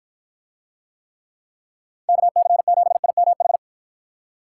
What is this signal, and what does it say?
Morse code, heard as a keyed tone, about 53 wpm, 700 Hz. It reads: KC7IMH